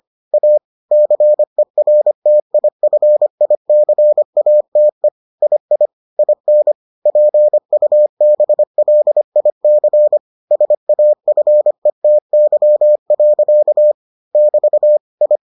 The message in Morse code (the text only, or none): A CERTIFICATE II IN PUBLIC SAFETY. <BT> I